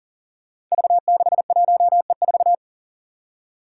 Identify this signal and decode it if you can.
Morse code, heard as a keyed tone, about 40 wpm, 700 Hz. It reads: V61E4
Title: Five Challenging Letter-Number Combinations 40wpm